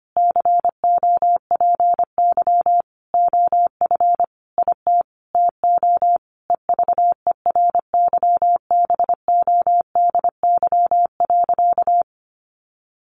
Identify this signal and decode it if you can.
Morse code, heard as a keyed tone, about 25 wpm, 700 Hz. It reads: COPY OF IT TO EVERYBODY.
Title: Sentences from Top 1400 Words 25wpm